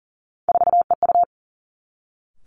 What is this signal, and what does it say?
Morse code, heard as a keyed tone, about 41 wpm, 690 Hz. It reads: 4EU
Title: Three Challenging Letter-Number Combinations 40wpm